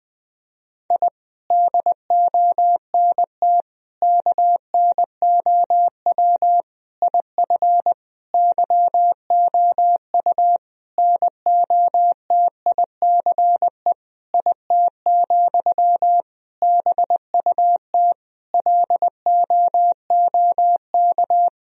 Morse code, heard as a keyed tone, about 20 wpm, 700 Hz. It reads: I DONT KNOW IF YOU NOTICE IT, BUT LOOK
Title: Sentences from Top 1000 Words - Review Part 2 - 20wpm